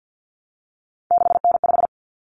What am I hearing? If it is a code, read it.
Morse code, heard as a keyed tone, about 50 wpm, 700 Hz. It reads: 6N5
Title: Common Cut Numbers - 3-Digit 50wpm